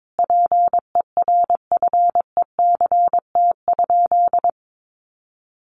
Morse code, heard as a keyed tone, about 22 wpm, 700 Hz. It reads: PERFECT?
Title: Sentences from Top 1000 Words 22wpm